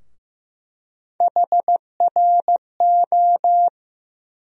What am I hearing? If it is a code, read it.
Morse code, heard as a keyed tone, about 15 wpm, 700 Hz. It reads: HRO
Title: USA Call Sign Postfixes - 3 Letters 15wpm